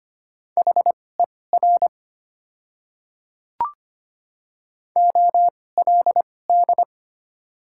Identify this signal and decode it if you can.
Morse code, heard as a keyed tone, about 25 wpm, spaced as for 18 wpm, 700 Hz. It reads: HER OLD
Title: 2x Character Spacing - Top 300 Words - 25wpm